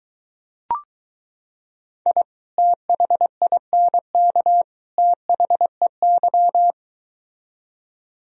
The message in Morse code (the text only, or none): I THINK THEY